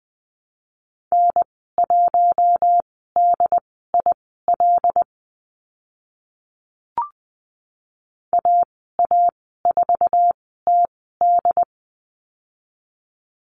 Morse code, heard as a keyed tone, about 20 wpm, spaced as for 14 wpm, 700 Hz. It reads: N1DIL AA4TD
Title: 082 Lesson 14g - Number 2 - Call Signs 20wpm